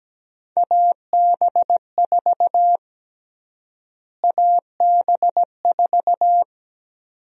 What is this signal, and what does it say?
Morse code, heard as a keyed tone, about 17 wpm, 700 Hz. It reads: AB4 AB4
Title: USA Call Sign Prefixes - Repeated 2x - 17wpm